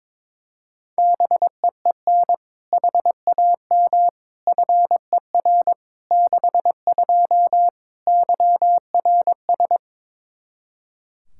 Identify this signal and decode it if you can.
Morse code, heard as a keyed tone, about 22 wpm, 700 Hz. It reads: BEEN HAM FER 62 YRS
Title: K1USN SST QSO Practice 22wpm